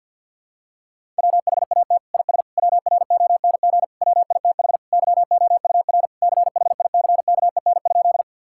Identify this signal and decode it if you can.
Morse code, heard as a keyed tone, about 51 wpm, 690 Hz. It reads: WHAT IS WRONG WITH YOUR CHICKEN?
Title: Sentences from Top 1000 Words - Review Part 5 - 50wpm